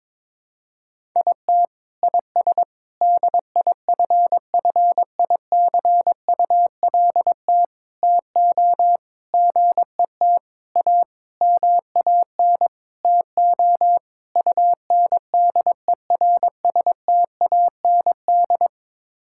Morse code, heard as a keyed tone, about 22 wpm, 700 Hz. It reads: IT IS DIFFICULT TO GET A MAN TO UNDERSTAND
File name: Sets of 9 Words - 22wpm